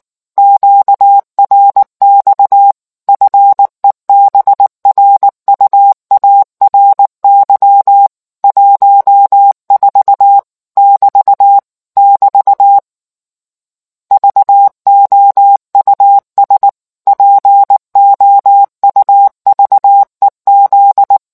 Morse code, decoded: QRX FEBRUARY 14 <BT> <BT> VOUS POUVEZ

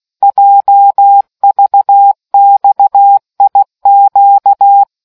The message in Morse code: JVXIQ

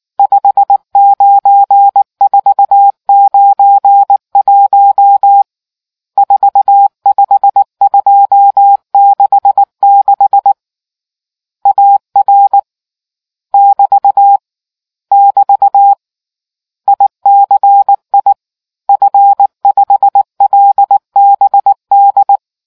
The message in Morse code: 59491 45266 AR <BT> <BT> ICI F5LBD

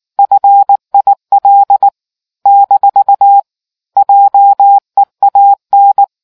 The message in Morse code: FIL - JEAN

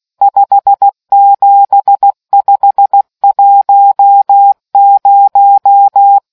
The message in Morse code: 57510

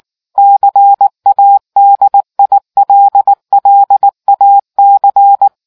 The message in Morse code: CADILLAC

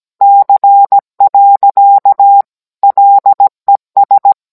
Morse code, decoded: C. LES

